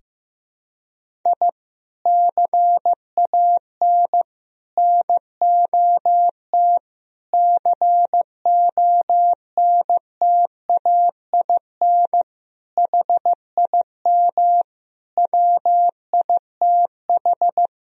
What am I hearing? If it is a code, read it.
Morse code, heard as a keyed tone, about 15 wpm, 700 Hz. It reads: I CAN NOT CONTAIN HIM WITH
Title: Sentences from Top 1000 Words - Review Part 3 - 15wpm